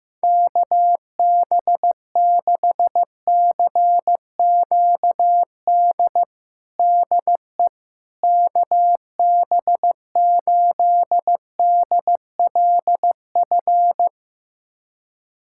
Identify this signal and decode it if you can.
Morse code, heard as a keyed tone, about 16 wpm, 700 Hz. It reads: KB6CQD DE KB8DLF
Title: USA Callsign DE Callsign 15wpm